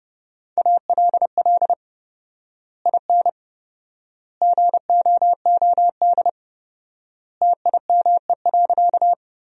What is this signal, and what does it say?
Morse code, heard as a keyed tone, about 32 wpm, 700 Hz. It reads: ALL IN GOOD TIME.
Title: 4x Word-Spacing - Sentences from Top 200 Words - 30wpm